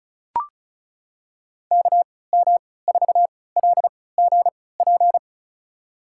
Morse code, heard as a keyed tone, about 36 wpm, spaced as for 20 wpm, 700 Hz. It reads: KM4LGP
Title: USA Call Signs - 6 Characters - 3x Character Spacing - 35wpm